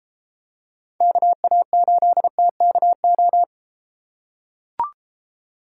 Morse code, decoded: KA8TKO

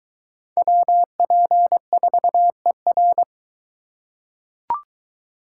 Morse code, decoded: WP4ER

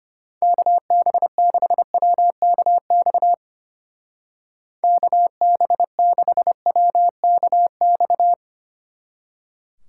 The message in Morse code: KB6WKX KB6WKX

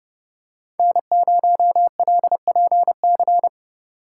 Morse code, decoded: N0LPC

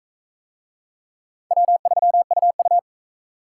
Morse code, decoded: W3UU

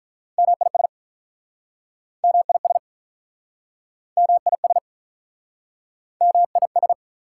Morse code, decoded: MIS MIS MIS MIS

